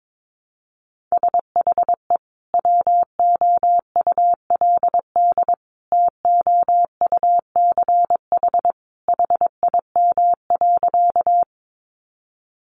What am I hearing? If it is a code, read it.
Morse code, heard as a keyed tone, about 22 wpm, 700 Hz. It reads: SHE WOULD TOUCH HIM.
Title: Sentences from Top 1200 Words 22wpm